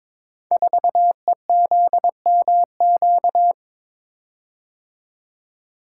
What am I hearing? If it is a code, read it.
Morse code, heard as a keyed tone, about 22 wpm, 700 Hz. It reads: 4EZMQ
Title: Code Groups - 5 Letter:Numbers - 22wpm